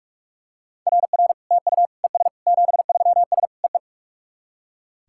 Morse code, heard as a keyed tone, about 44 wpm, 690 Hz. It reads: RR TU ES 73S EE
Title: SOTA QSO Practice 45wpm